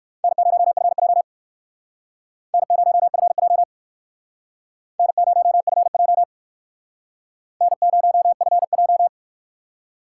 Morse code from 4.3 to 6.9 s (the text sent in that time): N0FJ